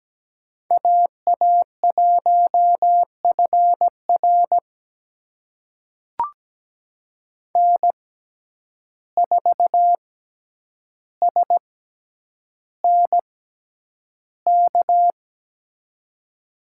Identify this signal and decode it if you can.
Morse code, heard as a keyed tone, about 17 wpm, spaced as for 9 wpm, 700 Hz. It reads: AA1FR N4SNK
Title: USA Call Signs - 5 Characters - 6x Character Spacing - 17wpm